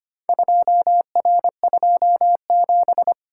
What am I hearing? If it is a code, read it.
Morse code, heard as a keyed tone, about 25 wpm, 700 Hz. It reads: 2R27
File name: practice / 25wpm / Four Letter-Number Combinations - 25wpm